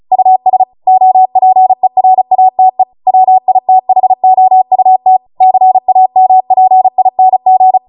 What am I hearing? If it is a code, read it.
Morse code, heard as a keyed tone, about 36 wpm, 750 Hz. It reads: US OPERATE WITHOUT CAMPING